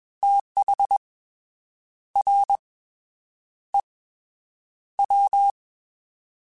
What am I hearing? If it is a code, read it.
Morse code, heard as a keyed tone, about 21 wpm, 800 Hz. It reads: TH R E W
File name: TH / TH mp3s / Fast TH @20